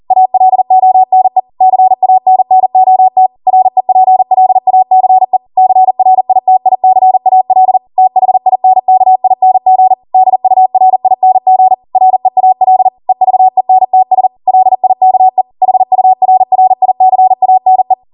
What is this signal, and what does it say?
Morse code, heard as a keyed tone, about 40 wpm, 750 Hz. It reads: ALONE CANNOT REPLACE CRITICAL THINKING DURING REAL EVENTS LIKE HURRICANE